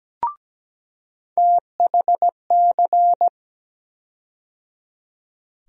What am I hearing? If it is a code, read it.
Morse code, heard as a keyed tone, about 17 wpm, 700 Hz. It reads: THC
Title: USA Call Sign Postfixes - 3 Letters 17wpm